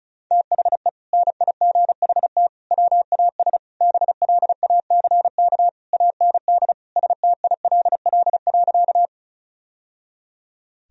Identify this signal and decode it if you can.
Morse code, heard as a keyed tone, about 34 wpm, 700 Hz. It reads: THE NIGHT WAS BLACK AND STILL.
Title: Sentences from Top 1000 Words - Review Part 2 - 35wpm